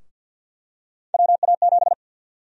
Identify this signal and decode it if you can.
Morse code, heard as a keyed tone, about 47 wpm, 700 Hz. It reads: WA7
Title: USA Call Sign Prefixes - Repeated 2x - 50wpm